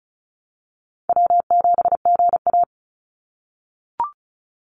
Morse code, heard as a keyed tone, about 36 wpm, 700 Hz. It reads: W7GA
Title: NA SOTA Chaser Callsigns 35wpm